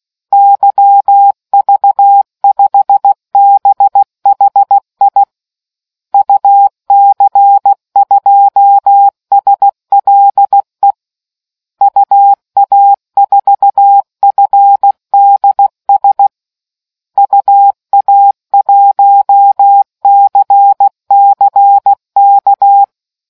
Morse code, heard as a keyed tone, about 16 wpm, 800 Hz. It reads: YV5BHI UC2SLE UA4FDS UA1CCK